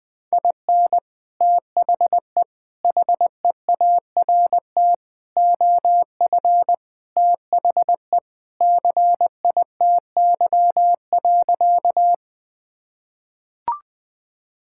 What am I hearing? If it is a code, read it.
Morse code, heard as a keyed tone, about 21 wpm, 700 Hz. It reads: IN THE HEART OF THE CITY.
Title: Sentences with 3-Word Phrases - Spoken Beginning - 20wpm